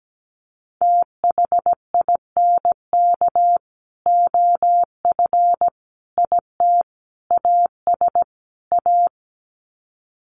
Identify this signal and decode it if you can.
Morse code, heard as a keyed tone, about 17 wpm, 700 Hz. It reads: THINK OF IT AS A
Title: Sets of 5 Words Encore2 17wpm